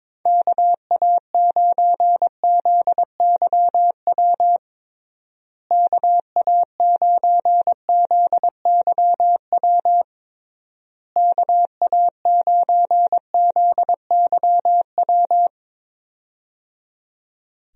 Morse code, heard as a keyed tone, about 22 wpm, 700 Hz. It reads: KA9ZYW KA9ZYW KA9ZYW
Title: USA Call Signs - 6 Characters - Repeated 3x - 22wpm